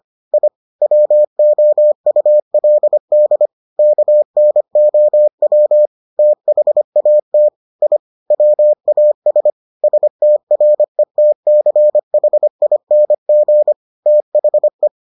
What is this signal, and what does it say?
Morse code, heard as a keyed tone, about 25 wpm, 600 Hz. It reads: I WOULD KNOW THAT I WAS STRETCHING THE